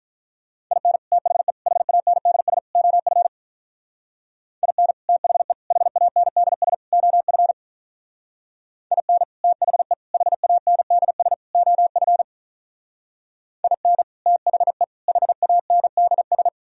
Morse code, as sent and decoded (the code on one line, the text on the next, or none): .. -. / - .... / .... .- -. -.. ... / --- ..-. / .. -. / - .... / .... .- -. -.. ... / --- ..-. / .. -. / - .... / .... .- -. -.. ... / --- ..-. / .. -. / - .... / .... .- -. -.. ...
IN THE HANDS OF IN THE HANDS OF IN THE HANDS OF IN THE HANDS